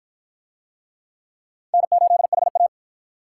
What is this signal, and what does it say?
Morse code, heard as a keyed tone, about 49 wpm, 700 Hz. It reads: N8HA